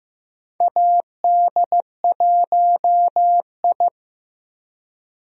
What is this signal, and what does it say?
Morse code, heard as a keyed tone, about 15 wpm, 700 Hz. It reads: AD1I
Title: USA Call Signs - 4 Characters - Repeated 2x - 15wpm